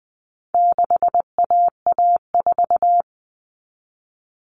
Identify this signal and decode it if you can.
Morse code, heard as a keyed tone, about 20 wpm, 700 Hz. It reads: 6AA4
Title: All Cut Numbers - 4-Digit 20wpm